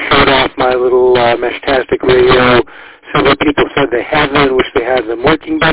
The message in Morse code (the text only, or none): none